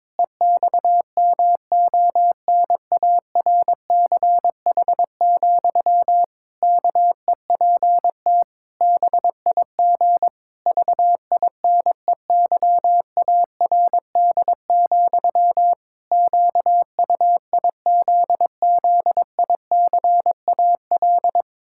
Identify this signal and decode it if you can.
Morse code, heard as a keyed tone, about 22 wpm, 700 Hz. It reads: EXMONARCH, KEPT BIG VINEYARD, QUIZZICAL
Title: Pangram Challenge v2 - 22wpm